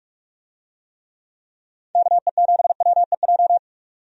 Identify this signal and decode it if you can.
Morse code, heard as a keyed tone, about 44 wpm, 700 Hz. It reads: KE7WEJ
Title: USA Call Signs - 6 Characters - 45wpm